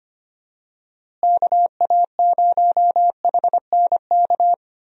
Morse code, heard as a keyed tone, about 25 wpm, 700 Hz. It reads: KA0HNK
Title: USA Call Signs - 6 Characters - Repeated 3x - 25wpm